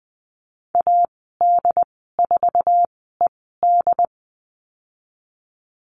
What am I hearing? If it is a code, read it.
Morse code, heard as a keyed tone, about 20 wpm, spaced as for 14 wpm, 700 Hz. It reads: AD4ED
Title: 069 Lesson 12g - Letter d - Call Signs 20wpm